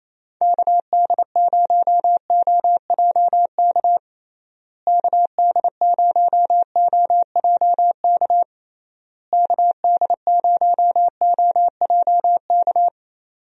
Morse code, heard as a keyed tone, about 28 wpm, 700 Hz. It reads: KD0OJK KD0OJK KD0OJK
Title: USA Call Signs - 6 Characters - Repeated 3x - 28wpm